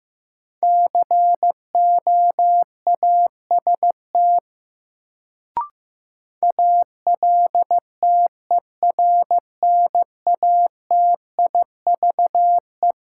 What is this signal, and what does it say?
Morse code, heard as a keyed tone, about 15 wpm, 700 Hz. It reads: COAST ALTERNATIVE
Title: Sentences from Top 1000 Words - Review Part 4 - New Word Drill - 15wpm